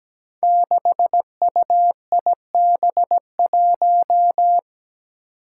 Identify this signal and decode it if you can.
Morse code, heard as a keyed tone, about 17 wpm, 700 Hz. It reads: 6UIB1
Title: Five Challenging Letter-Number Combinations 17wpm